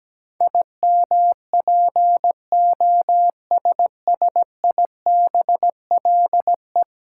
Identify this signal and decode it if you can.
Morse code, heard as a keyed tone, about 17 wpm, 700 Hz. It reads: IMPOSSIBLE
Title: Sentences from Top 1100 Words - New Word Drill - 17wpm